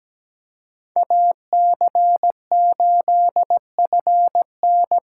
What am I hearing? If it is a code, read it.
Morse code, heard as a keyed tone, about 20 wpm, 700 Hz. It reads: AC8FN
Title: Speed Racing - USA 2x2 Call Signs 17wpm